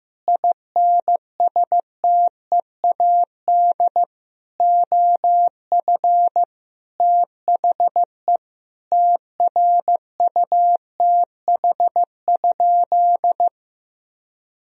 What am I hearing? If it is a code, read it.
Morse code, heard as a keyed tone, about 15 wpm, 700 Hz. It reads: INSTEAD OF THE TRUTH?
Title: Sentences from Top 1000 Words - Review Part 4 - 15wpm